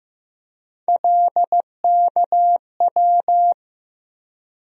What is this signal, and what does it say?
Morse code, heard as a keyed tone, about 15 wpm, 700 Hz. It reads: LKW
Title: USA Call Sign Postfixes - 3 Letters 15wpm